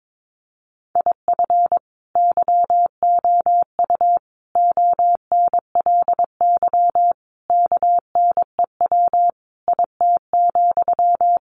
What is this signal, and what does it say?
Morse code, heard as a keyed tone, about 22 wpm, 700 Hz. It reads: IF YOU ONLY KNEW IT,
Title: Sentences from Top 1500 Words 22wpm